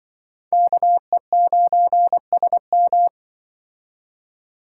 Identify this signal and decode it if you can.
Morse code, heard as a keyed tone, about 24 wpm, 700 Hz. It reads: KE9SM
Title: Speed Racing - USA 2x2 Call Signs 20wpm